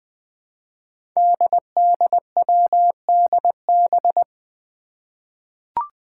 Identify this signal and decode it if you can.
Morse code, heard as a keyed tone, about 20 wpm, 700 Hz. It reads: DDWDB
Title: Five Challenging Letter-Number Combinations 20wpm